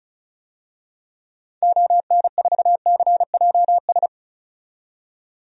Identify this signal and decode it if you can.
Morse code, heard as a keyed tone, about 34 wpm, 700 Hz. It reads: ON4CJS
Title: DXCC-100 Call Signs - 6 Characters - 35wpm